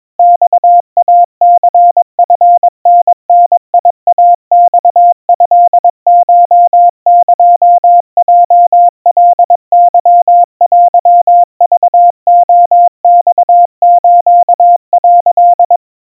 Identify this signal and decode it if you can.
Morse code (Japanese Wabun code), heard as a keyed tone, about 22 wpm, 700 Hz. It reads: マイニチタダイマトコエヲカケテクレマス」
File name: 80 cpm 700 Hz Crispy